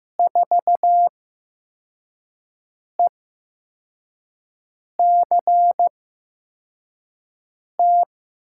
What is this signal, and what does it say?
Morse code, heard as a keyed tone, about 15 wpm, spaced as for 4 wpm, 700 Hz. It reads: 4ECT
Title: USA Call Signs - 6 Characters - 8x Character Spacing - 15wpm